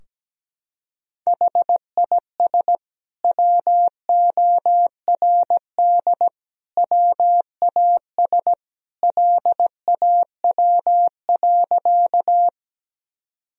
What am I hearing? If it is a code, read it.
Morse code, heard as a keyed tone, about 17 wpm, 700 Hz. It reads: HIS WORD WAS LAW.
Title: Sentences from Top 1000 Words - Review Part 3 - 17wpm